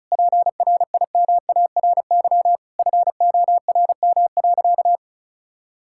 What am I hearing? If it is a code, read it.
Morse code, heard as a keyed tone, about 34 wpm, 700 Hz. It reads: PRIMARY FORM.
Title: Sentences from Top 1000 Words - Review Part 5 - 35wpm